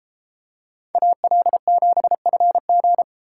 Morse code, heard as a keyed tone, about 33 wpm, 700 Hz. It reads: AL7FG